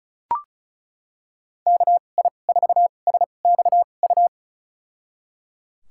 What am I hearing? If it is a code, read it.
Morse code, heard as a keyed tone, about 36 wpm, spaced as for 25 wpm, 700 Hz. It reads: KI4SXU
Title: USA Call Signs - 6 Characters - 2x Character Spacing - 35wpm